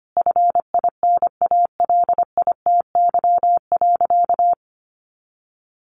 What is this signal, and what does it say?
Morse code, heard as a keyed tone, about 25 wpm, 700 Hz. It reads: FINALITY.
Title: A Sound Of Thunder 25wpm